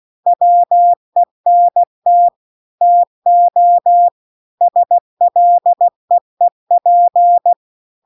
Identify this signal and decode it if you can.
Morse code, heard as a keyed tone, about 16 wpm, 700 Hz. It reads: WENT TO SLEEP